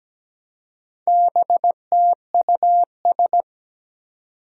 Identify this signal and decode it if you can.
Morse code, heard as a keyed tone, about 17 wpm, 700 Hz. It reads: BTUS
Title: Four Challenging Letter-Number Combinations 17wpm